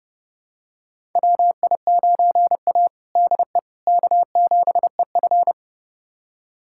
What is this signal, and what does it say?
Morse code, heard as a keyed tone, about 30 wpm, 700 Hz. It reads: WI9A DE K7EF
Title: USA Callsign DE Callsign 30wpm